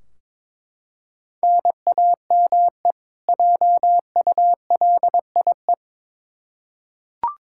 Morse code, heard as a keyed tone, about 22 wpm, 700 Hz. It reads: NAME JULIE
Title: K1USN SST QSO Practice 22wpm